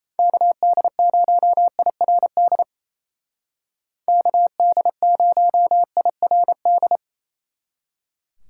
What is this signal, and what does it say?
Morse code, heard as a keyed tone, about 36 wpm, 700 Hz. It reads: KD0IRD KD0IRD